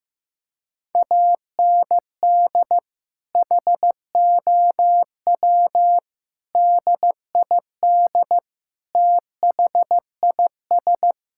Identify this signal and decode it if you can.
Morse code, heard as a keyed tone, about 15 wpm, 700 Hz. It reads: AND HOW DID THIS
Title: Sentences from Top 1000 Words - Review Part 6 - 15wpm